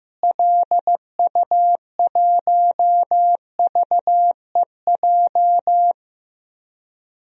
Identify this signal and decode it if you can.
Morse code, heard as a keyed tone, about 16 wpm, 700 Hz. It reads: LU1VEJ
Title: DXCC-100 Call Signs - 6 Characters - Repeated 3x - 15wpm